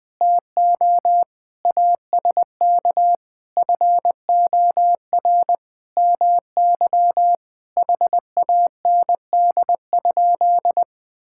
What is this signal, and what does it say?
Morse code, heard as a keyed tone, about 20 wpm, 700 Hz. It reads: TO ASK FOR MY HAND?
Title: Sentences from Top 1600 Words 20wpm